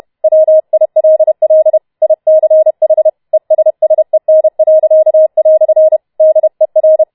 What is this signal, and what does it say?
Morse code, heard as a keyed tone, about 31 wpm, 630 Hz. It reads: WILL ICH ESSEN." DER